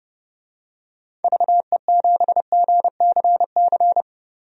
Morse code, heard as a keyed tone, about 30 wpm, 700 Hz. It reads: VE7GCC
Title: DXCC-100 Call Signs - 6 Characters - 30wpm